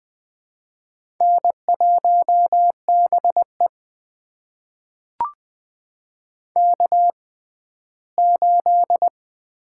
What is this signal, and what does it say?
Morse code, heard as a keyed tone, about 21 wpm, spaced as for 14 wpm, 700 Hz. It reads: N1BE K8
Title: USA Call Signs - 4 Characters - 6x Character Spacing - 20wpm